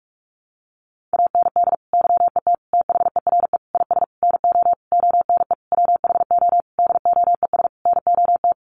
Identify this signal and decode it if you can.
Morse code, heard as a keyed tone, about 44 wpm, 700 Hz. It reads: AND YET THERE IS NO ONE WHO DOES NOT
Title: Sentences from Top 1000 Words 45wpm